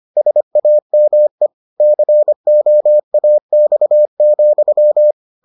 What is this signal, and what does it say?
Morse code, heard as a keyed tone, about 25 wpm, 600 Hz. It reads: SAME COAX,